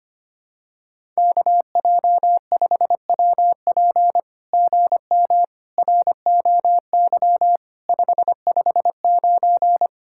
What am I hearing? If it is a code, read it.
Morse code, heard as a keyed tone, about 24 wpm, 700 Hz. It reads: KJ5WP GM ROY 559
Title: SOTA QSO Practice 25wpm